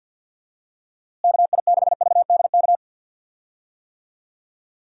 Morse code, heard as a keyed tone, about 50 wpm, 700 Hz. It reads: KI6VDK
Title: USA Call Signs - 6 Characters - Repeated 2x - 50wpm